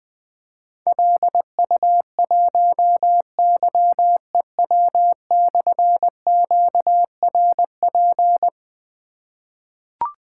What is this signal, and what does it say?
Morse code, heard as a keyed tone, about 20 wpm, 700 Hz. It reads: LU1YEW/QRP
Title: Difficult Call Signs 20wpm